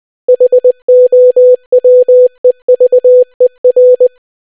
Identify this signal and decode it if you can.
Morse code, heard as a keyed tone, about 20 wpm, 500 Hz. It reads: HOWEVER